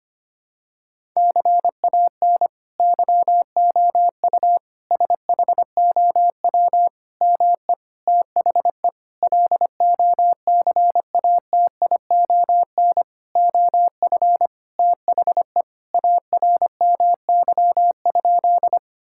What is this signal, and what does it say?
Morse code, heard as a keyed tone, about 25 wpm, 700 Hz. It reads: CAN YOU SHOW ME THE LOCATION OF THE ARMY?
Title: Sentences from Top 1000 Words - Review Part 6 - 25wpm